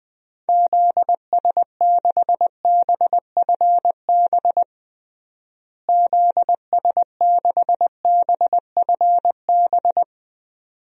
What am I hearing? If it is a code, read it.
Morse code, heard as a keyed tone, about 20 wpm, 700 Hz. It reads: ZS6BFB ZS6BFB
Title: DXCC-100 Call Signs - 6 Characters - Repeated 2x - 20wpm